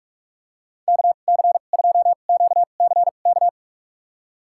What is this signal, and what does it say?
Morse code, heard as a keyed tone, about 42 wpm, spaced as for 31 wpm, 700 Hz. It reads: KC2QCK
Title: USA Call Signs - 6 Characters - 2x Character Spacing - 45wpm